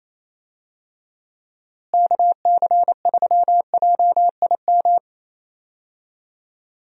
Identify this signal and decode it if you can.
Morse code, heard as a keyed tone, about 28 wpm, 700 Hz. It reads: KC3JIM
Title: USA Call Signs - 6 Characters - Repeated 3x - 28wpm